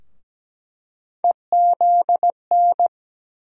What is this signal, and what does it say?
Morse code, heard as a keyed tone, about 17 wpm, 700 Hz. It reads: EZN
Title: USA Call Sign Postfixes - 3 Letters 17wpm